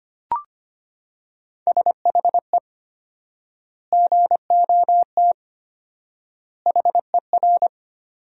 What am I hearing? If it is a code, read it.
Morse code, heard as a keyed tone, about 25 wpm, 700 Hz. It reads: SHE GOT HER
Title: 4x Word-Spacing - Sentences from Top 500 Words - 25wpm